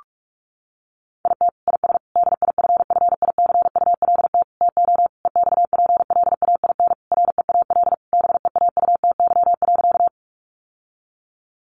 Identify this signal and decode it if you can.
Morse code, heard as a keyed tone, about 44 wpm, 700 Hz. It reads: IT IS DIFFICULT TO EXPLAIN REAL BEAUTY.
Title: Sentences from Top 1100 Words 45wpm